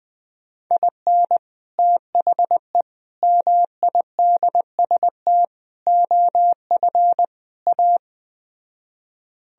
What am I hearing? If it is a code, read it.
Morse code, heard as a keyed tone, about 20 wpm, 700 Hz. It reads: IN THE MIDST OF A